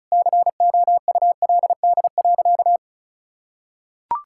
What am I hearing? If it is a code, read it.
Morse code, heard as a keyed tone, about 36 wpm, 700 Hz. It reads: COULD.
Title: Sentences from Top 1600 Words 35wpm